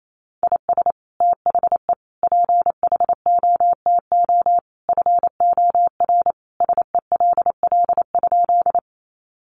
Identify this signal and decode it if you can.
Morse code, heard as a keyed tone, about 28 wpm, 700 Hz. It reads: IS THE PHOTO FOR SELL?
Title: Sentences from Top 1200 Words 28wpm